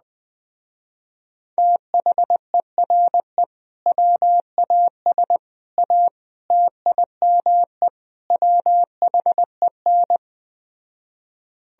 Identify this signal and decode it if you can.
Morse code, heard as a keyed tone, about 20 wpm, 700 Hz. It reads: THERE WAS A TIME WHEN
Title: Speed-Racing - Sets of 5 Words 17wpm